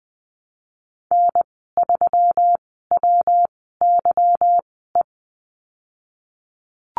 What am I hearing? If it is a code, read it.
Morse code, heard as a keyed tone, about 20 wpm, spaced as for 15 wpm, 700 Hz. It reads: N3WYE
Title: 154 Lesson 25g - Letter p - Call Signs 20wpm